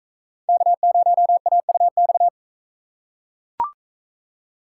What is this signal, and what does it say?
Morse code, heard as a keyed tone, about 37 wpm, 700 Hz. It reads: K0AUX